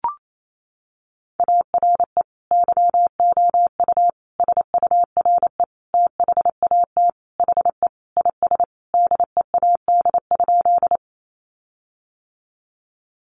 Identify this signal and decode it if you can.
Morse code, heard as a keyed tone, about 28 wpm, 700 Hz. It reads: ARE YOU SURE THAT HE IS DEAD?
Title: Sentences from Top 1500 Words 28wpm